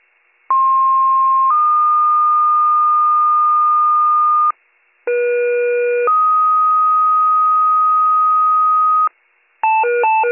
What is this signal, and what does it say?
Two-tone page, heard as a one-second tone, then a three-second tone, about 1060 then 1220 Hz, about 480 then 1250 Hz. It is heard from 0.5 to 9.1 s.